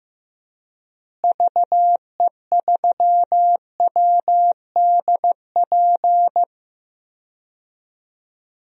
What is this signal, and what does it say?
Morse code, heard as a keyed tone, about 15 wpm, 700 Hz. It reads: VE3WDP
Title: DXCC-100 Call Signs - 6 Characters - Repeated 3x - 15wpm